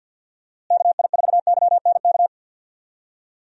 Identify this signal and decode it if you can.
Morse code, heard as a keyed tone, about 52 wpm, 700 Hz. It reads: KI4YNK
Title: USA Call Signs - 6 Characters - Repeated 3x - 50wpm